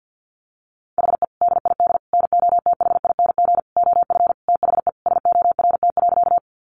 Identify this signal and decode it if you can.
Morse code, heard as a keyed tone, about 50 wpm, 700 Hz. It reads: HE DID NOTHING OF THE SORT.
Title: Sentences from Top 1300 Words 50wpm